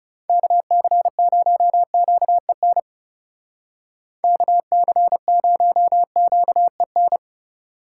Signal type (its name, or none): Morse code